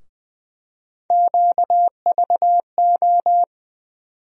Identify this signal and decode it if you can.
Morse code, heard as a keyed tone, about 21 wpm, 700 Hz. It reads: QVO
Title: USA Call Sign Postfixes - 3 Letters 20wpm